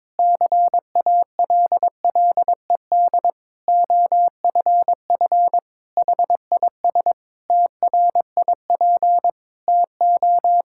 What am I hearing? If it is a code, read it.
Morse code, heard as a keyed tone, about 22 wpm, 700 Hz. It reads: CALLED OFF HIS TRIP TO
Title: Pangram Challenge v2 - 22wpm